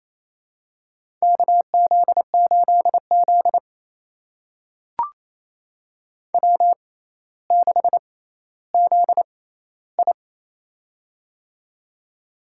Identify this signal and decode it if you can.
Morse code, heard as a keyed tone, about 28 wpm, spaced as for 14 wpm, 700 Hz. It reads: KZ8Z W6ZI
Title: USA Call Signs - 4 Characters - 6x Character Spacing - 28wpm